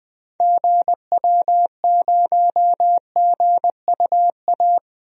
Morse code, heard as a keyed tone, about 21 wpm, 700 Hz. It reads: GW0GUA
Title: DXCC-100 Call Signs - 6 Characters - Repeated 2x - 20wpm